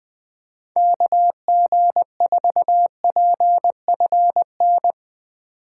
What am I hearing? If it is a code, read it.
Morse code, heard as a keyed tone, about 20 wpm, 700 Hz. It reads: KG4PFN
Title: USA Call Signs - 6 Characters - 3x Character Spacing - 20wpm